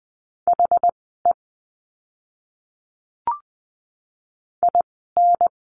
Morse code, heard as a keyed tone, about 20 wpm, spaced as for 15 wpm, 700 Hz. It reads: HE IN
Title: 139 Lesson 23f - Letter f - Words 20wpm